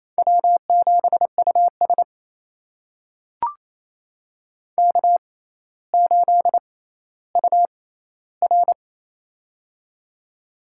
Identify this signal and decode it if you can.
Morse code, heard as a keyed tone, about 28 wpm, spaced as for 14 wpm, 700 Hz. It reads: W7US K8UR
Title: USA Call Signs - 4 Characters - 6x Character Spacing - 28wpm